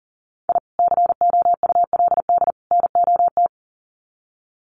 Morse code, heard as a keyed tone, about 39 wpm, 700 Hz. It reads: I COULD NOT
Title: Sets of 3 Words 40wpm